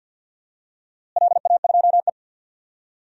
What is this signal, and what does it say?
Morse code, heard as a keyed tone, about 53 wpm, 700 Hz. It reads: LA2E
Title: DXCC-100 Call Signs - 4 Characters - Repeated 3x - 50wpm